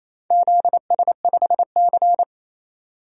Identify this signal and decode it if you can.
Morse code, heard as a keyed tone, about 28 wpm, 700 Hz. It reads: ZS5C
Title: DXCC-100 Call Signs - 4 Characters - 28wpm